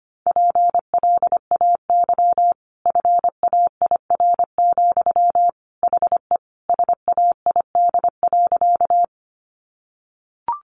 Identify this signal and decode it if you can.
Morse code, heard as a keyed tone, about 25 wpm, 700 Hz. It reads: PLAY FAIR, HE SAID.
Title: Sentences from Top 1300 Words 25wpm